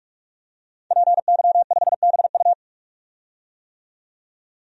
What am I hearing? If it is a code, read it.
Morse code, heard as a keyed tone, about 44 wpm, 690 Hz. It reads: PY5BU
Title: DXCC-100 Call Signs - 5 Characters - 45wpm